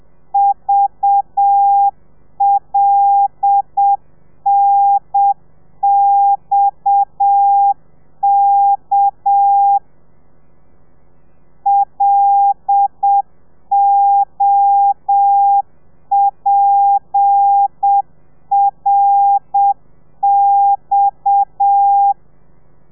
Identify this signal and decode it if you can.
Morse code, heard as a keyed tone, about 7 wpm, 800 Hz. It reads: VLNXK LOPRX
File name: telegrafia / i2tfj / Lez7 30